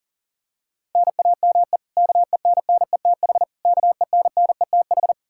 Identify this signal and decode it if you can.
Morse code, heard as a keyed tone, about 37 wpm, 700 Hz. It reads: NAME KENNETH KENNETH
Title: Practice QSO - Exchange 1 - Beginning - 40wpm